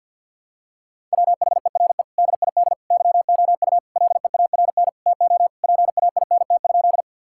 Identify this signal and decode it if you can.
Morse code, heard as a keyed tone, about 53 wpm, 700 Hz. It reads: WHERE DID YOU LEARN TO PAINT?
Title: Sentences from Top 1000 Words - Review Part 6 - 50wpm